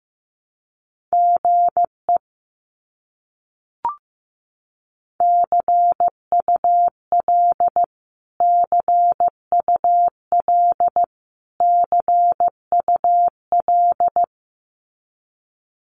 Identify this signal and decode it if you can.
Morse code, heard as a keyed tone, about 15 wpm, 700 Hz. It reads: GE CUL CUL CUL
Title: Abbreviations Repeated 3x - 15wpm